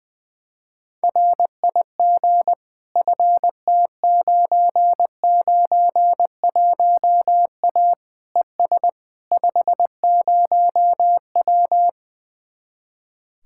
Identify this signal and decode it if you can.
Morse code, heard as a keyed tone, about 20 wpm, 700 Hz. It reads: RIG FT991A ES 50W
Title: Speed-Racing Common QSO Phrases Encore - 20wpm